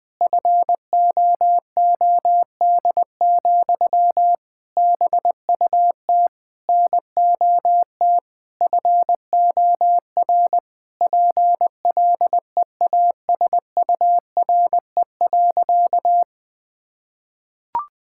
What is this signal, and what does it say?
Morse code, heard as a keyed tone, about 20 wpm, 700 Hz. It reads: FOOD, BUT NOT FOR PLEASURE.
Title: Sentences from Top 1600 Words 20wpm